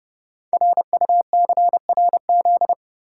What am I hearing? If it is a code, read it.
Morse code, heard as a keyed tone, about 30 wpm, 700 Hz. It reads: RUCRZ
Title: Code Groups - 5 Letters - 30wpm